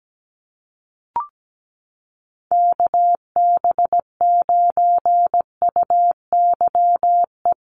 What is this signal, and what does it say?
Morse code, heard as a keyed tone, about 17 wpm, 700 Hz. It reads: KB9UYE
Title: USA Call Signs Repeated 3x 17wpm